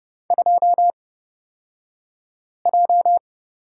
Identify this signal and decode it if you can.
Morse code, heard as a keyed tone, about 31 wpm, spaced as for 5 wpm, 700 Hz. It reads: 2J
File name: Lesson 37d - Letter x - Two Characters 30wpm